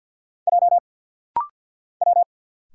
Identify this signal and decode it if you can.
Morse code, heard as a keyed tone, about 50 wpm, 700 Hz. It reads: J W
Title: Single Letter-Number Alt - Rapid-Fire 50wpm